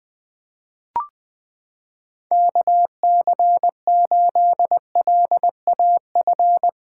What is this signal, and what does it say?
Morse code, heard as a keyed tone, about 20 wpm, 700 Hz. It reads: KC8LAF
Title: USA Callsign DE Callsign - 6 Characters - 20wpm